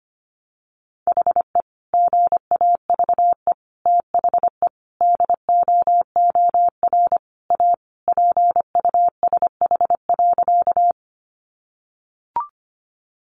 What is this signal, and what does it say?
Morse code, heard as a keyed tone, about 25 wpm, 700 Hz. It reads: HE GAVE THE DOOR A PUSH.
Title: Sentences from Top 1300 Words 25wpm